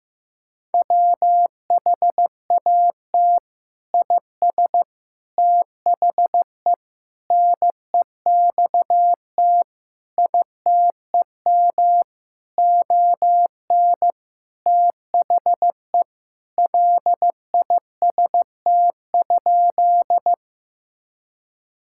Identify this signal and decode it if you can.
Morse code, heard as a keyed tone, about 15 wpm, 700 Hz. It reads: WHAT IS THE NEXT ITEM ON THE LIST?
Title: Sentences from Top 1000 Words - Review Part 6 - 15wpm